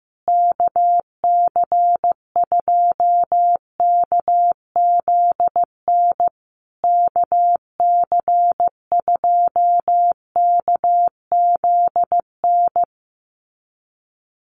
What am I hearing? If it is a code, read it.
Morse code, heard as a keyed tone, about 14 wpm, 700 Hz. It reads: KC2KZN KC2KZN
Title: USA Call Signs Repeated 3x 15wpm